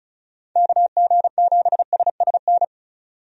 Morse code, heard as a keyed tone, about 34 wpm, 700 Hz. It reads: KG7SSN
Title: USA Call Signs - 6 Characters - 35wpm